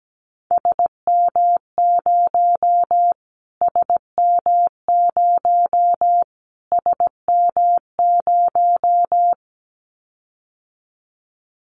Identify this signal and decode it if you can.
Morse code, heard as a keyed tone, about 17 wpm, 700 Hz. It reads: SM0 SM0 SM0
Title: Callsign Prefixes for Top 25 DXCC - Repeated 3x - 17wpm